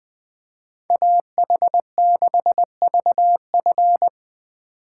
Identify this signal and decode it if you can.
Morse code, heard as a keyed tone, about 25 wpm, 700 Hz. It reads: AH6VF
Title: Speed Racing - USA Call Signs 20wpm